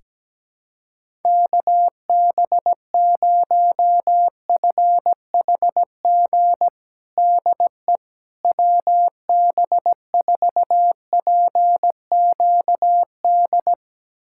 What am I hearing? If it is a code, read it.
Morse code, heard as a keyed tone, about 17 wpm, 700 Hz. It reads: KB0FHG DE WB4PQD
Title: USA Callsign DE Callsign - 6 Characters - 17wpm